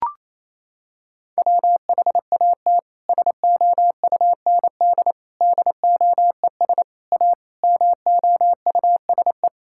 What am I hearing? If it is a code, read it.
Morse code, heard as a keyed tone, about 27 wpm, 700 Hz. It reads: WHAT SOUND DOES A MOUSE
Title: Sentences from Top 1000 Words - Review Part 5 - 28wpm